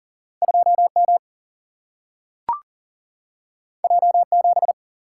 Morse code, heard as a keyed tone, about 40 wpm, 700 Hz. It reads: 2M J7
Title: Single-Letter Single-Number 40wpm